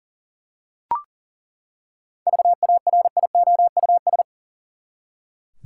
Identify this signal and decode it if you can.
Morse code, heard as a keyed tone, about 40 wpm, 700 Hz. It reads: VARIOUS
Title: Seven Letter Words - 40wpm